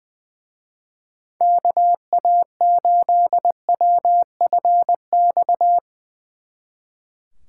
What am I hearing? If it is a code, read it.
Morse code, heard as a keyed tone, about 23 wpm, 700 Hz. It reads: KA8WFX
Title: Speed Racing - USA 2x3 Call Signs 20wpm